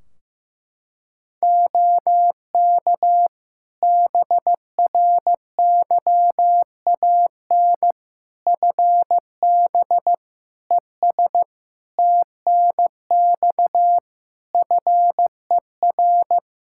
Morse code, heard as a keyed tone, about 15 wpm, 700 Hz. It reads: OK BRYAN FB ES TNX FER
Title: Practice QSO - Exchange 2 - Info - 15wpm